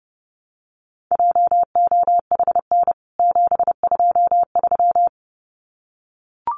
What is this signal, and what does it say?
Morse code, heard as a keyed tone, about 30 wpm, 700 Hz. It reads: JOHN 723
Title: CWT Exchange 30wpm